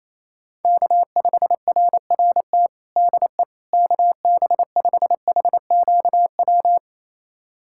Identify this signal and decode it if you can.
Morse code, heard as a keyed tone, about 28 wpm, 700 Hz. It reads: K5RRT DE KB5HQW
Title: USA Callsign DE Callsign 28wpm